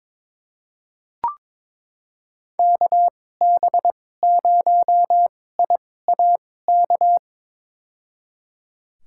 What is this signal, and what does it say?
Morse code, heard as a keyed tone, about 22 wpm, spaced as for 16 wpm, 700 Hz. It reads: KB0IAK
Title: USA Call Signs - 6 Characters - 2x Character Spacing - 22wpm